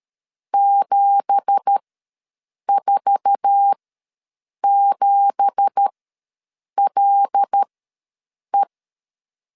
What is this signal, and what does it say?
Morse code, heard as a keyed tone, about 13 wpm, spaced as for 7 wpm, 800 Hz. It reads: 747LE
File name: LEZIONI / 9 2867 ws 12wpm 3431